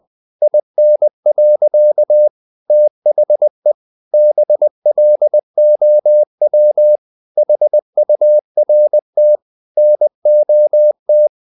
Morse code, heard as a keyed tone, about 20 wpm, 600 Hz. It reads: IN. THE BLOW HURT NOT
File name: the old man and the sea part 2 20wpm